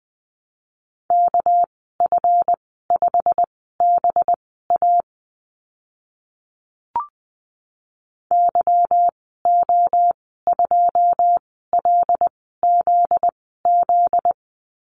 Morse code, heard as a keyed tone, about 20 wpm, spaced as for 14 wpm, 700 Hz. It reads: KF5BA YO2LZZ
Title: 245 Lesson 39g - Letter z - Call Signs 20wpm